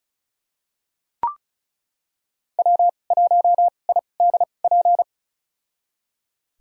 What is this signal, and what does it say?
Morse code, heard as a keyed tone, about 36 wpm, spaced as for 25 wpm, 700 Hz. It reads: W1IDP
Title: USA Call Signs - 5 Characters - 2x Character Spacing - 35wpm